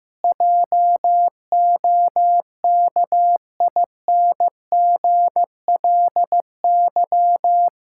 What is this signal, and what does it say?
Morse code, heard as a keyed tone, about 15 wpm, 700 Hz. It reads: JOKINGLY
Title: Pangram Challenge v2 - 15wpm